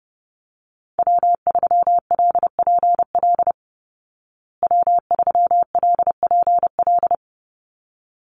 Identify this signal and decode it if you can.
Morse code, heard as a keyed tone, about 30 wpm, 700 Hz. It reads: W3LPL W3LPL
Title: Call Signs Repeated 3x - 30wpm